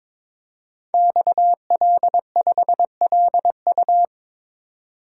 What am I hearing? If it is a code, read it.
Morse code, heard as a keyed tone, about 22 wpm, 700 Hz. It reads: XL5LU
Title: Code Groups - 5 Letter:Numbers - 22wpm